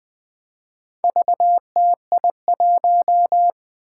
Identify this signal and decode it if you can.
Morse code, heard as a keyed tone, about 20 wpm, 700 Hz. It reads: VTI1
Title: Four Challenging Letter-Number Combinations 20wpm